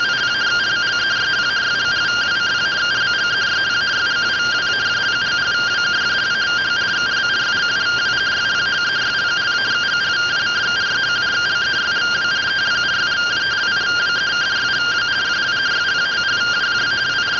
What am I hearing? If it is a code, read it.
Two-tone teleprinter signal, about 170 Hz shift.